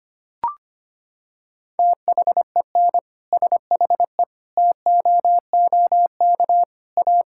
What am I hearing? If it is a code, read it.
Morse code, heard as a keyed tone, about 25 wpm, 700 Hz. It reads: THEN SHE TOOK A
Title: Sentences from Top 1600 Words 25wpm